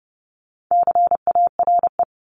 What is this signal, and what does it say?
Morse code, heard as a keyed tone, about 30 wpm, 700 Hz. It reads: CARE
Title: Top 500 Words 30wpm